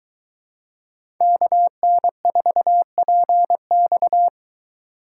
Morse code, heard as a keyed tone, about 22 wpm, 700 Hz. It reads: KN4PX